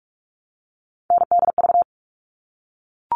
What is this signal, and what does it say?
Morse code, heard as a keyed tone, about 44 wpm, 700 Hz. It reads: NDV
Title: All Cut Numbers - 3-Digit 45wpm